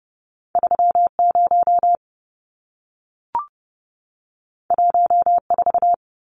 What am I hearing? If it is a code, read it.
Morse code, heard as a keyed tone, about 30 wpm, 700 Hz. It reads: 30 14
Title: Numbers - 2-Digit 30wpm